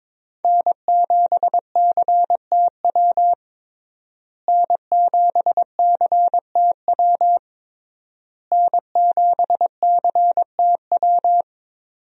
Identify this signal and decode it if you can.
Morse code, heard as a keyed tone, about 22 wpm, 700 Hz. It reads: N7CTW N7CTW N7CTW
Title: USA Call Signs - 5 Characters - Repeated 3x - 22wpm